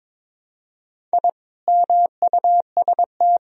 Morse code, heard as a keyed tone, about 22 wpm, 700 Hz. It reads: I MUST
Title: Sentences from Top 1000 Words - Review Part 1 - 22wpm